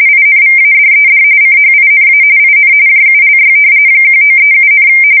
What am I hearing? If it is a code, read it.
RTTY, about 45.45 baud. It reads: INVADED CHINA IN JULY 1937,
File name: fsk170 / battleshipnj